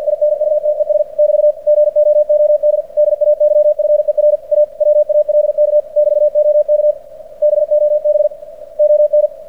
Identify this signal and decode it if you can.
Morse code, heard as a keyed tone, about 48 wpm, 600 Hz. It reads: INUTES <BT> GOOD DAYFER A WALK <BT> OK BOB ON
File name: archived / CRSnet